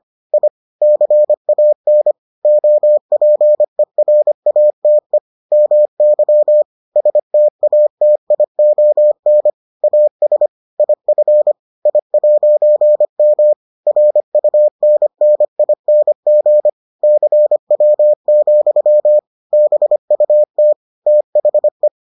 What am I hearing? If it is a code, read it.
Morse code, heard as a keyed tone, about 25 wpm, 600 Hz. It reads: I CAN OPERATE MY STATION AS IF I'M RUNNING CW, BUT THE